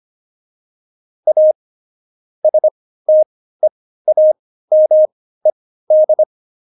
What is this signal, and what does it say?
Morse code, heard as a keyed tone, about 25 wpm, spaced as for 15 wpm, 620 Hz. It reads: A STEAMED
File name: Jokes-25-char-15-eff